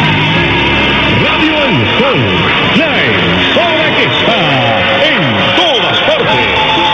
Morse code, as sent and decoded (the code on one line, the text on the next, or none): none